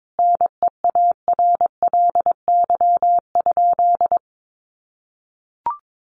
Morse code, heard as a keyed tone, about 22 wpm, 700 Hz. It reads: NEARLY?
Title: Sentences from Top 1300 Words 22wpm